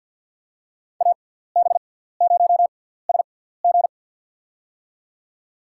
Morse code, heard as a keyed tone, about 49 wpm, spaced as for 17 wpm, 690 Hz. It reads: AB0SG
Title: USA Call Signs - 5 Characters - 6x Character Spacing - 50wpm